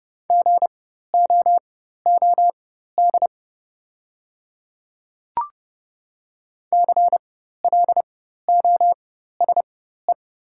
Morse code, heard as a keyed tone, about 30 wpm, spaced as for 14 wpm, 700 Hz. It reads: GOOD CLOSE
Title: 4x Character Spacing - Top 500 Words - 30wpm